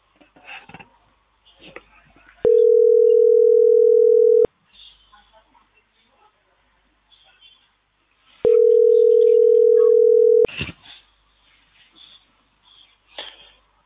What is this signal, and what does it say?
Ringback tone (North America), 440 + 480 Hz.